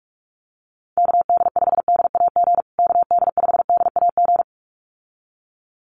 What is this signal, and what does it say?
Morse code, heard as a keyed tone, about 44 wpm, 700 Hz. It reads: KD5DAG KD5DAG
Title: USA Call Signs Repeated 2x 45wpm